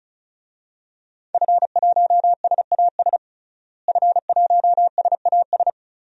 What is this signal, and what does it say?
Morse code, heard as a keyed tone, about 34 wpm, 700 Hz. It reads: F1SAS F1SAS
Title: DXCC-100 Call Signs - 5 Characters - Repeated 3x - 35wpm